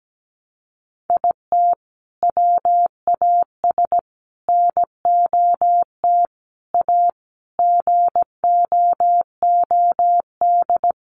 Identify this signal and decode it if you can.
Morse code, heard as a keyed tone, about 17 wpm, 700 Hz. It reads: IT WAS NOT A GOOD
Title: Sentences from Top 1100 Words 17wpm